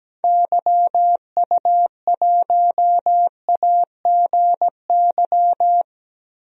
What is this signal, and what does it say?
Morse code, heard as a keyed tone, about 17 wpm, 700 Hz. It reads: YU1AGY
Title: DXCC-100 Call Signs - 6 Characters - 17wpm